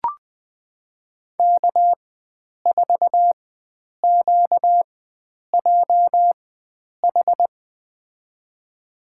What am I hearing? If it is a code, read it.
Morse code, heard as a keyed tone, about 20 wpm, spaced as for 9 wpm, 700 Hz. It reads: K4QJH
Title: USA Call Signs - 5 Characters - 4x Character Spacing - 20wpm